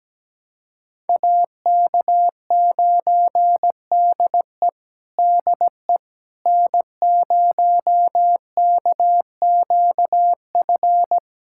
Morse code, heard as a keyed tone, about 17 wpm, 700 Hz. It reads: AK9DE DE N0KQF
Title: USA Callsign DE Callsign - 5 Characters - 17wpm